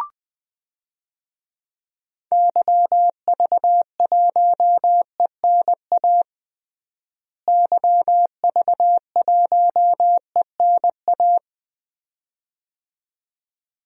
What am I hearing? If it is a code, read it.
Morse code, heard as a keyed tone, about 20 wpm, 700 Hz. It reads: YV1ENA YV1ENA
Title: DXCC-100 Call Signs - 6 Characters - Repeated 2x - 20wpm